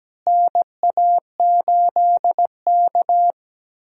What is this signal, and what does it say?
Morse code, heard as a keyed tone, about 17 wpm, 700 Hz. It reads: NA8K
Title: USA Call Signs - 4 Characters - Repeated 3x - 17wpm